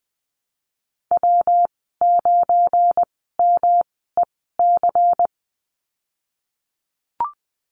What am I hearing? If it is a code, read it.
Morse code, heard as a keyed tone, about 20 wpm, spaced as for 15 wpm, 700 Hz. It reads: W9MEC
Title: 173 Lesson 28g - Number 9 - Call Signs 20wpm